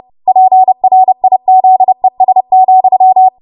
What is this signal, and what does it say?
Morse code, heard as a keyed tone, about 30 wpm, 750 Hz. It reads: PRIZES,
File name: Morse / 30WPM